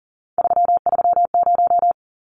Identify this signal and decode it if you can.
Morse code, heard as a keyed tone, about 40 wpm, 700 Hz. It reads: 330
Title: Numbers - 3-Digit 40wpm